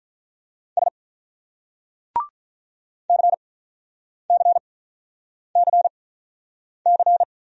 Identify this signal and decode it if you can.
Morse code, heard as a keyed tone, about 42 wpm, spaced as for 9 wpm, 700 Hz. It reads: S CCCC
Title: Speed-Racing Single Letters 35wpm